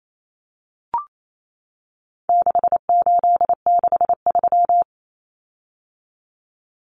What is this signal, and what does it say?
Morse code, heard as a keyed tone, about 28 wpm, 700 Hz. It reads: 6863
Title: Common Cut Numbers - 4-Digit 28wpm